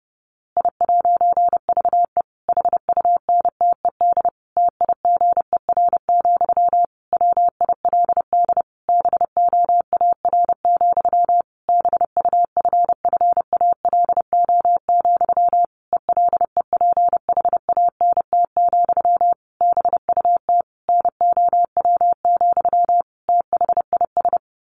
Morse code, heard as a keyed tone, about 30 wpm, 700 Hz. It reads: I'VE HUNTED TIGER, WILD BOAR, BUFFALO, ELEPHANT, BUT NOW, THIS
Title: A Sound Of Thunder 30wpm